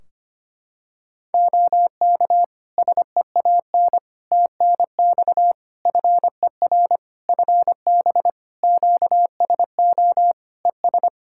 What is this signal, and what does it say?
Morse code, heard as a keyed tone, about 25 wpm, 700 Hz. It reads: OK SEAN TNX FER FB QSO ES
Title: Practice QSO - Exchange 4 - Ending - 25wpm